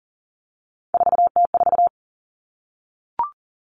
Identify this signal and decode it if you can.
Morse code, heard as a keyed tone, about 40 wpm, 700 Hz. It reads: 4T4
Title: Common Cut Numbers - 3-Digit 40wpm